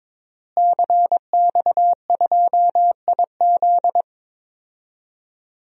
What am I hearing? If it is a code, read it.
Morse code, heard as a keyed tone, about 21 wpm, 700 Hz. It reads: CX2IZ
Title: DXCC-100 Call Signs - 5 Characters - Repeated 2x - 22wpm